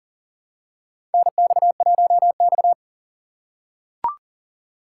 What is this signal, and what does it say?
Morse code, heard as a keyed tone, about 40 wpm, 700 Hz. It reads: NX1X
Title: USA Call Signs - 4 Characters - 3x Character Spacing - 40wpm